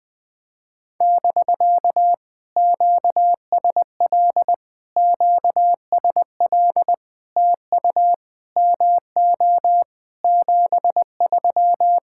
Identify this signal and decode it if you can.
Morse code, heard as a keyed tone, about 20 wpm, 700 Hz. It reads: <BK> QSL QSL TU MO 73
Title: POTA QSO Practice - 20wpm